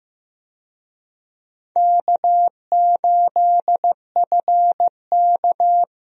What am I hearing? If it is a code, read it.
Morse code, heard as a keyed tone, about 15 wpm, 700 Hz. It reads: K8FK
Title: USA Call Signs - 4 Characters - Repeated 2x - 15wpm